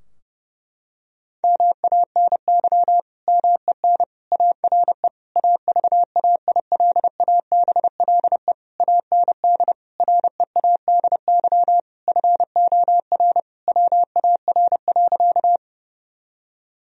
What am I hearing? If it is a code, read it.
Morse code, heard as a keyed tone, about 30 wpm, 700 Hz. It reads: MANY MEN ARE AVAILABLE AND READY FOR WAR.
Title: Sentences from Top 1000 Words - Review Part 3 - 30wpm